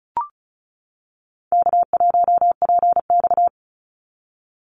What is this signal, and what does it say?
Morse code, heard as a keyed tone, about 36 wpm, 700 Hz. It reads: K1PX
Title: Call Signs - 35wpm